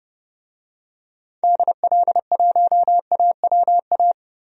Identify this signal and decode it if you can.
Morse code, heard as a keyed tone, about 30 wpm, 700 Hz. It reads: DL1AWA
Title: DXCC-100 Call Signs - 6 Characters - Repeated 3x - 30wpm